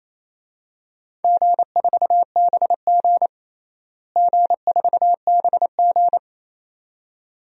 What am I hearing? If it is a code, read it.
Morse code, heard as a keyed tone, about 28 wpm, 700 Hz. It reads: G4BG G4BG
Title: DXCC-100 Call Signs - 4 Characters - Repeated 2x - 28wpm